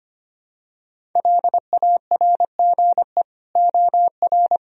Morse code, heard as a keyed tone, about 25 wpm, 700 Hz. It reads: LARGE OR
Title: Sentences from Top 1000 Words - Review Part 1 - 25wpm